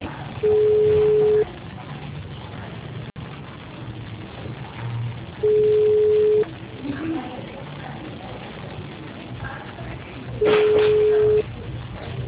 Ringback tone (Europe, 425 Hz).
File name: out- -2007- - - .18277